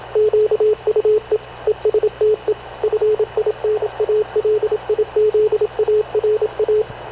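Morse code, heard as a keyed tone, about 27 wpm, 430 Hz, with radio noise. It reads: QUE ESTE FINALIZARA
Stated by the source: Havana Radio CLA press (PX) broadcast, recorded by Radio Officer Paul Dumesnil aboard mv CNAV QUEST/CZDO.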